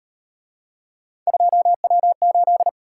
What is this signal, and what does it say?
Morse code, heard as a keyed tone, about 36 wpm, 700 Hz. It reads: 2W8